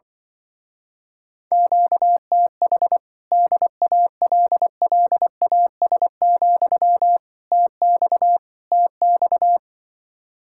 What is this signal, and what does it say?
Morse code, heard as a keyed tone, about 25 wpm, 700 Hz. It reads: QTH DALLAS, TX TX
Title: Speed-Racing Common QSO Phrases Encore - 20wpm